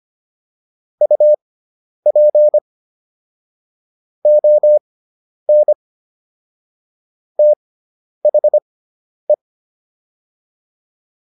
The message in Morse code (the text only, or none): UP ON THE